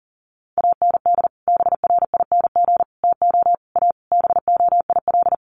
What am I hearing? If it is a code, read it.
Morse code, heard as a keyed tone, about 40 wpm, 700 Hz. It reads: AND BRING TO A BOIL
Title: Sets of 5 Words Encore 40wpm